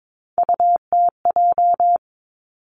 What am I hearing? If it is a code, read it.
Morse code, heard as a keyed tone, about 22 wpm, 700 Hz. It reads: UTJ
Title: Three Challenging Letter-Number Combinations 22wpm